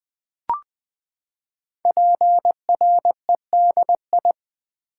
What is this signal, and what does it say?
Morse code, heard as a keyed tone, about 20 wpm, 700 Hz. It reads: PREDI
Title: Build A Word - 5 Letters At A Time - 10-Letter Words - 20wpm